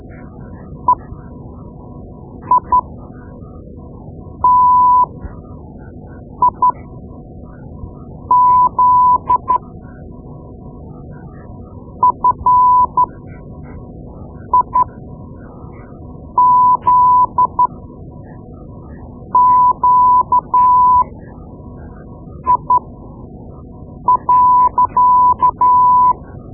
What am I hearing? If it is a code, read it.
Morse code, heard as a keyed tone, about 12 wpm, spaced as for 5 wpm, 1000 Hz. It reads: EITIZ FIZQI.
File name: Morse Lessons / Morse Lessons / Tape-2B